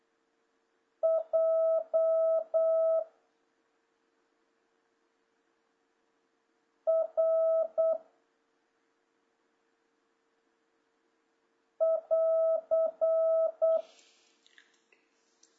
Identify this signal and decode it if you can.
Morse code, heard as a keyed tone, about 8 wpm, 650 Hz. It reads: J R <AR>
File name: jh8cbh / 2026cw